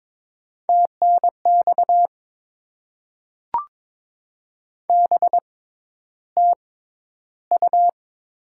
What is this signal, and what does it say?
Morse code, heard as a keyed tone, about 22 wpm, spaced as for 11 wpm, 700 Hz. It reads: TNX BTU
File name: Top 20 QSO Elements - 6x Character Spacing - 22wpm